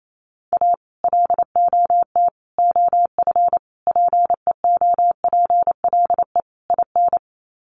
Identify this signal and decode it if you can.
Morse code, heard as a keyed tone, about 28 wpm, 700 Hz. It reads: A LOT OF PEOPLE IN
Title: Sets of 5 Words 28wpm